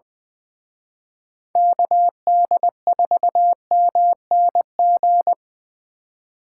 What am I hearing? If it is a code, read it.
Morse code, heard as a keyed tone, about 24 wpm, 700 Hz. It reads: KD4MNG
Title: Speed Racing - USA 2x3 Call Signs 20wpm